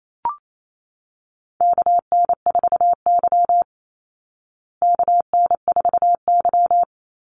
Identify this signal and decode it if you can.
Morse code, heard as a keyed tone, about 28 wpm, 700 Hz. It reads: KN4Y KN4Y
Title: Call Signs Repeated 3x - 28wpm